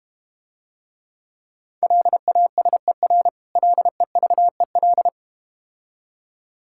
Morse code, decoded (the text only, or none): LASER LEVEL